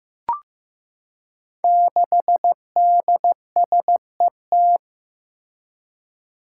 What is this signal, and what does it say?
Morse code, heard as a keyed tone, about 15 wpm, 700 Hz. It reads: 6DSET
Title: Five Challenging Letter-Number Combinations 15wpm